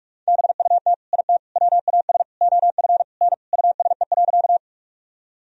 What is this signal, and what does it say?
Morse code, heard as a keyed tone, about 43 wpm, 700 Hz. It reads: BUT IT WAS OF N USE.
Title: Sentences from Top 100 Words and Abbreviations 45wpm